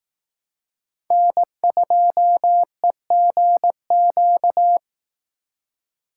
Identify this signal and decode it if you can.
Morse code, heard as a keyed tone, about 18 wpm, 700 Hz. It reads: N2EGQ
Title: Speed Racing - USA 1x3 Call Signs 15wpm